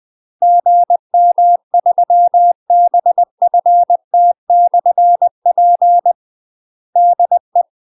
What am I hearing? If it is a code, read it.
Morse code, heard as a keyed tone, about 20 wpm, 700 Hz. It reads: GM3BFT/P DE
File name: UK MASTER QSOs TRAINING / 20WPM / UK MASTER QSOs 0023